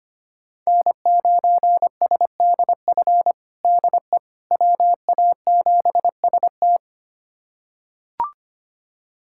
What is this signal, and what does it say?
Morse code, heard as a keyed tone, about 25 wpm, 700 Hz. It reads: N9SDF DE WA7ST
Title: USA Callsign DE Callsign - 5 Characters - 25wpm